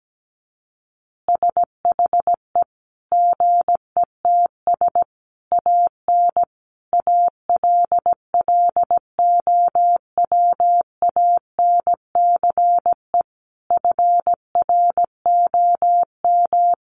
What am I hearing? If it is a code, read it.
Morse code, heard as a keyed tone, about 17 wpm, 700 Hz. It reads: SHE GETS AN ALLOWANCE FROM
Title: Easy Phrases 17wpm